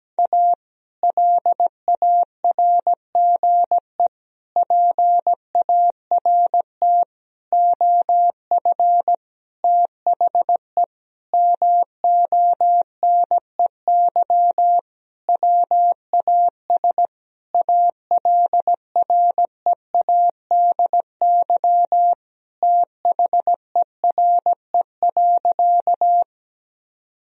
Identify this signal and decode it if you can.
Morse code, heard as a keyed tone, about 17 wpm, 700 Hz. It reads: A LARGE PART OF THE MONEY WAS ALREADY THERE.
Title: Sentences from Top 1000 Words - Review Part 1 - 17wpm